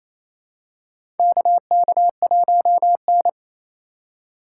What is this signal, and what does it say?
Morse code, heard as a keyed tone, about 28 wpm, 700 Hz. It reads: KK1N
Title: USA Call Signs - 4 Characters - 3x Character Spacing - 28wpm